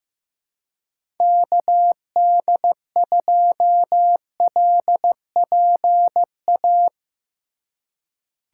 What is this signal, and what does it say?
Morse code, heard as a keyed tone, about 16 wpm, 700 Hz. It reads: KD2LPA
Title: USA Call Signs - 6 Characters - 4x Character Spacing - 15wpm